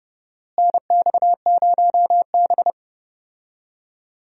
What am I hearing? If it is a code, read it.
Morse code, heard as a keyed tone, about 32 wpm, 700 Hz. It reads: NX0B
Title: USA Call Signs - 4 Characters - Repeated 3x - 30wpm